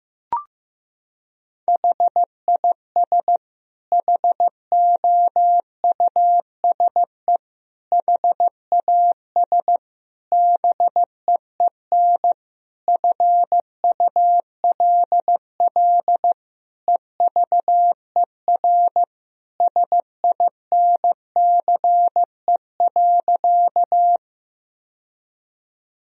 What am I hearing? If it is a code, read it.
Morse code, heard as a keyed tone, about 15 wpm, 700 Hz. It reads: HIS HOUSE HAS BEEN FULL EVER SINCE.
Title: Sentences from Top 1000 Words - Review Part 3 - 15wpm